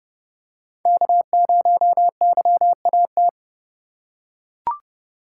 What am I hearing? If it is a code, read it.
Morse code, heard as a keyed tone, about 29 wpm, 700 Hz. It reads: K0YAT
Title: USA Call Signs - 5 Characters - 3x Character Spacing - 30wpm